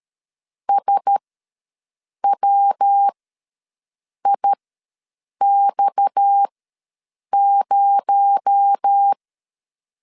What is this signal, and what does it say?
Morse code, heard as a keyed tone, about 13 wpm, spaced as for 6 wpm, 800 Hz. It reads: SWIX0
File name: LEZIONI / 8 YJ45 / 8 Y ws 12wpm 3409